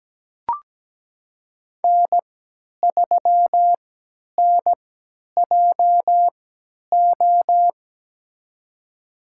Morse code, heard as a keyed tone, about 17 wpm, spaced as for 10 wpm, 700 Hz. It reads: N3NJO
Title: USA Call Signs - 5 Characters - 3x Character Spacing - 17wpm